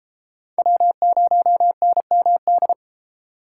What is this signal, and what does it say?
Morse code, heard as a keyed tone, about 33 wpm, 700 Hz. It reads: W0NMD